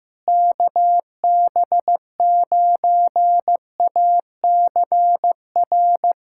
Morse code, heard as a keyed tone, about 15 wpm, 700 Hz. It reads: KB9ACR
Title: USA Callsign DE Callsign - 6 Characters - 15wpm